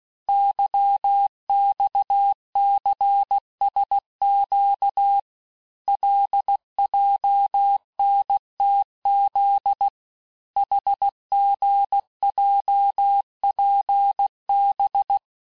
Morse code, heard as a keyed tone, about 16 wpm, 800 Hz. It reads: YXCSQ LJNTZ HGJPB
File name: cours-du-lundi files / cours-1592--teste-chiffre-vitesse-15mpm